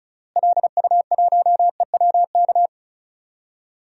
Morse code, heard as a keyed tone, about 36 wpm, 700 Hz. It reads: LU1EWK
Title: DXCC-100 Call Signs - 6 Characters - Repeated 3x - 35wpm